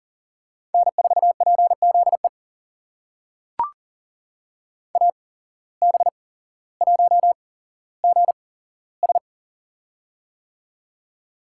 Morse code, heard as a keyed tone, about 40 wpm, spaced as for 17 wpm, 700 Hz. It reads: N4PZE AB1GS
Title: USA Call Signs - 5 Characters - 8x Character Spacing - 40wpm